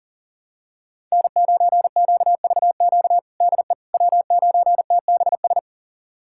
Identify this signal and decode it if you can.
Morse code, heard as a keyed tone, about 37 wpm, 700 Hz. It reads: N9QVQ DE W9TBS
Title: USA Callsign DE Callsign - 5 Characters - 40wpm